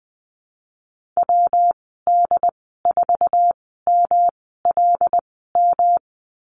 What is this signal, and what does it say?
Morse code, heard as a keyed tone, about 20 wpm, spaced as for 15 wpm, 700 Hz. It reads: WD4MLM
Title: 147 Lesson 24g - Letter y - Call Signs 20wpm